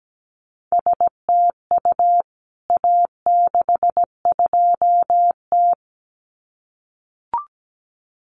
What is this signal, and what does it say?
Morse code, heard as a keyed tone, about 17 wpm, 700 Hz. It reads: STU A62T
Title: CWT Exchange 17wpm